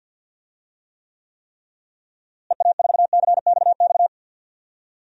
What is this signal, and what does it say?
Morse code, heard as a keyed tone, about 46 wpm, 700 Hz. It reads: EA4CXX
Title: DXCC-100 Call Signs - 6 Characters - Repeated 3x - 50wpm